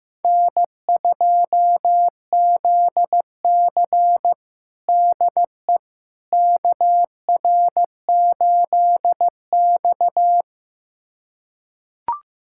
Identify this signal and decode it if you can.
Morse code, heard as a keyed tone, about 16 wpm, 700 Hz. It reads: N2ZC DE KR8X
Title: USA Callsign DE Callsign - 4 Characters - 15wpm